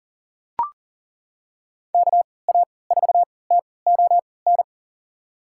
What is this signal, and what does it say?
Morse code, heard as a keyed tone, about 40 wpm, spaced as for 23 wpm, 700 Hz. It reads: KA4TON
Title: USA Call Signs - 3x Character Spacing - 40wpm